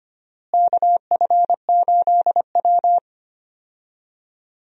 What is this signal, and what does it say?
Morse code, heard as a keyed tone, about 25 wpm, 700 Hz. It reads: KF8W
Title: USA Call Signs - 4 Characters - Repeated 3x - 25wpm